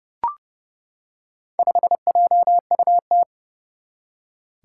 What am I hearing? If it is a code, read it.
Morse code, heard as a keyed tone, about 28 wpm, 700 Hz. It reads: 5JUT
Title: Four Challenging Letter-Number Combinations 30wpm